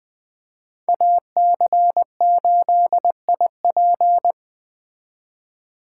Morse code, heard as a keyed tone, about 20 wpm, 700 Hz. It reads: AC8IP
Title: Speed Racing - USA 2x2 Call Signs 17wpm